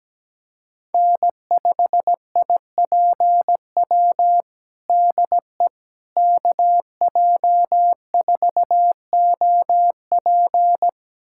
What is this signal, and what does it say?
Morse code, heard as a keyed tone, about 17 wpm, 700 Hz. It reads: N5IPW DE KJ4OP
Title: USA Callsign DE Callsign - 5 Characters - 17wpm